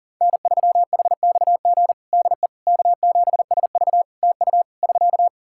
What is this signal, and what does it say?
Morse code, heard as a keyed tone, about 37 wpm, 700 Hz. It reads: N3HXG DE K7SV TU <SK>
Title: K1USN SST QSO Practice 40wpm